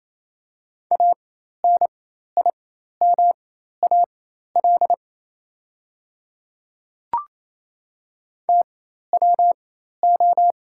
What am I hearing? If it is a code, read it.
Morse code, heard as a keyed tone, about 27 wpm, spaced as for 13 wpm, 690 Hz. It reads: ANIMAL TWO
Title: 4x Character Spacing - Top 200 Words - 28wpm